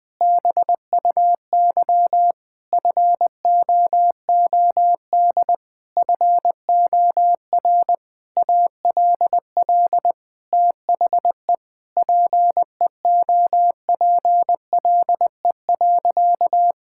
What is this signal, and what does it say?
Morse code, heard as a keyed tone, about 20 wpm, 700 Hz. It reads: BUY FOOD FOR ALL THE PEOPLE.
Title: Sentences from Top 1600 Words 20wpm